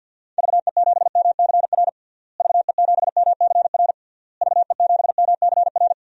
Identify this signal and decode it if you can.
Morse code, heard as a keyed tone, about 50 wpm, 700 Hz. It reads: VE7MCR VE7MCR VE7MCR
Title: DXCC-100 Call Signs - 6 Characters - Repeated 3x - 50wpm